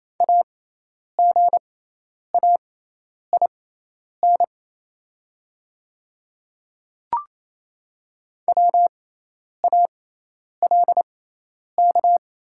Morse code, heard as a keyed tone, about 28 wpm, spaced as for 10 wpm, 690 Hz. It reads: AGAIN WALK
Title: 6x Character Spacing - Top 300 Words - 28wpm